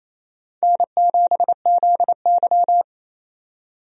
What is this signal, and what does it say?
Morse code, heard as a keyed tone, about 28 wpm, 700 Hz. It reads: N7ZY
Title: USA Call Signs - 4 Characters - 28wpm